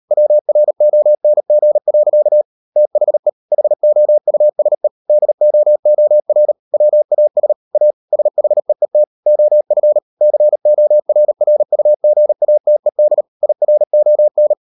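Morse code, heard as a keyed tone, about 38 wpm, 600 Hz. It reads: WRONG. THE HOUSE DOOR WAS A SHEET OF CORRUGATED IRON
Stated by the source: Morse code practice at 38 WPM, sending Ken Follett's novel Never in English, from CW med Gnister.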